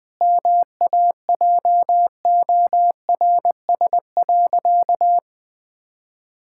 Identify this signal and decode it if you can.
Morse code, heard as a keyed tone, about 20 wpm, 700 Hz. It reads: MAJORS.
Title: Pangram Challenge v2 - 20wpm